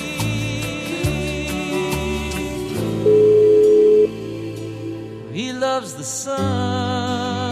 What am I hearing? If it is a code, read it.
Ringback tone (Europe, 425 Hz), with noise.